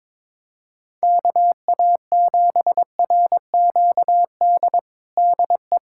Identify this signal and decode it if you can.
Morse code, heard as a keyed tone, about 21 wpm, 700 Hz. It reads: KA7RQD DE
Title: USA Callsign DE Callsign - 6 Characters - 22wpm